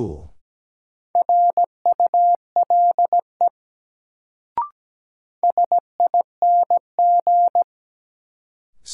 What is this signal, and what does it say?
Morse code, heard as a keyed tone, about 17 wpm, 700 Hz. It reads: RULE SING
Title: Sentences from Top 400 Words - New Word Drill - 17wpm